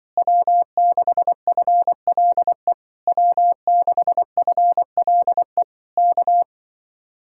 Morse code, heard as a keyed tone, about 25 wpm, 700 Hz. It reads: W6FLE W6FLE K